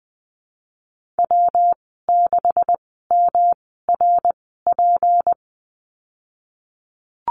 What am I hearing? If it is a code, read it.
Morse code, heard as a keyed tone, about 20 wpm, spaced as for 15 wpm, 700 Hz. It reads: W6MRP
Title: 173 Lesson 28g - Number 9 - Call Signs 20wpm